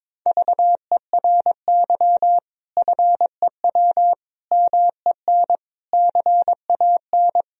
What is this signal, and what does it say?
Morse code, heard as a keyed tone, about 22 wpm, 700 Hz. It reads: VERY FEW MEN CAN
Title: Sentences from Top 1000 Words - Review Part 1 - 22wpm